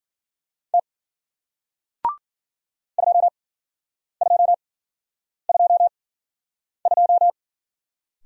Morse code, heard as a keyed tone, about 52 wpm, spaced as for 9 wpm, 700 Hz. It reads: T 2222